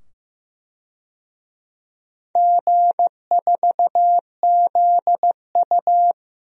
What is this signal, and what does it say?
Morse code, heard as a keyed tone, about 15 wpm, 700 Hz. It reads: G4ZU
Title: DXCC-100 Call Signs - 4 Characters - 15wpm